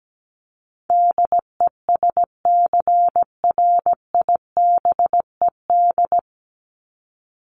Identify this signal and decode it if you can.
Morse code, heard as a keyed tone, about 17 wpm, 700 Hz. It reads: DESCRIBED
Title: ED Suffix - Top 100 Words with Base Words 17wpm